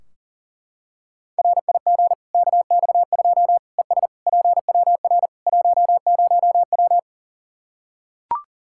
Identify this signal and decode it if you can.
Morse code, heard as a keyed tone, about 40 wpm, 700 Hz. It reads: RIG KX2 ES PWR 10W
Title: Practice QSO - Exchange 2 - Info - 40wpm